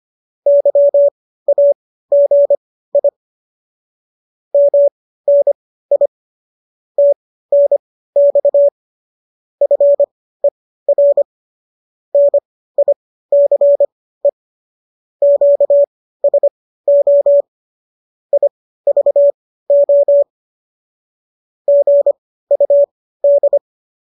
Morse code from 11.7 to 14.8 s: NICE

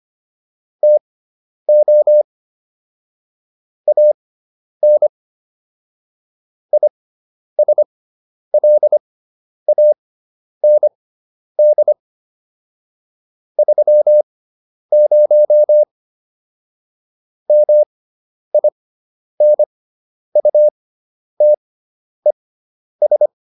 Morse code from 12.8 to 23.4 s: 30 MINUTES

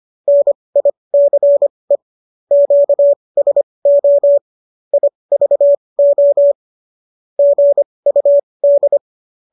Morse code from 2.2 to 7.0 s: QSO IVO

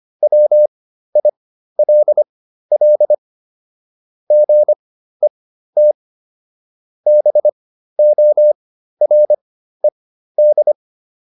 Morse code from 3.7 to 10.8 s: GET BORED

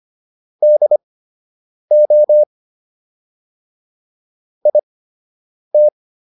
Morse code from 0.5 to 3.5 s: DO

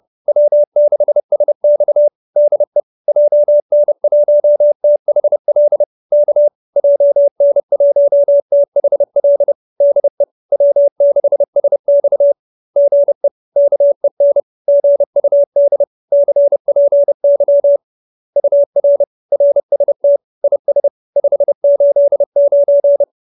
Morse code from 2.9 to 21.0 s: JN1THL K JN1THL DE W6SX GE KEN GUD CPY UR RST IS

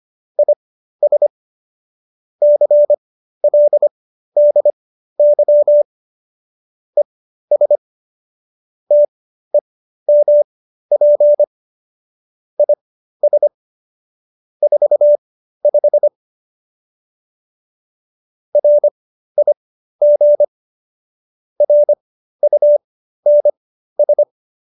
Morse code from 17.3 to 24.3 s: RIG RUNS

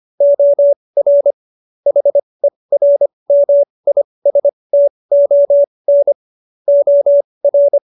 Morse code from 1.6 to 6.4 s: HERMISTON

